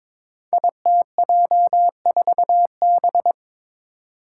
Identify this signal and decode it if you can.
Morse code, heard as a keyed tone, about 23 wpm, 700 Hz. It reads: ITJ4B
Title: Five Challenging Letter-Number Combinations 22wpm